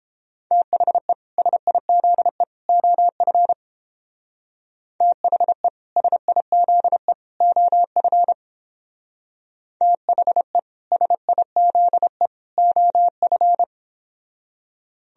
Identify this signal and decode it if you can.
Morse code, heard as a keyed tone, about 29 wpm, 700 Hz. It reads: THE SIZE OF THE SIZE OF THE SIZE OF